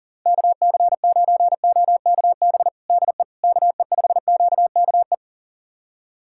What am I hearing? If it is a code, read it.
Morse code, heard as a keyed tone, about 40 wpm, 700 Hz. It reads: KC9OKB DE KE5QKE
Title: USA Callsign DE Callsign - 6 Characters - 40wpm